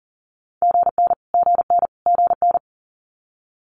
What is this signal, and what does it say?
Morse code, heard as a keyed tone, about 42 wpm, 700 Hz. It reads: GN GN GN
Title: Multi-letter Phonemes Repeated 3x 40wpm